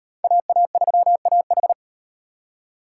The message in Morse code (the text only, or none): AA3AH